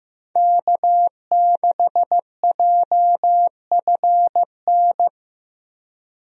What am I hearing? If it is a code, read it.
Morse code, heard as a keyed tone, about 15 wpm, 700 Hz. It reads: K6JFN
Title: USA Call Signs - 5 Characters - Repeated 2x - 15wpm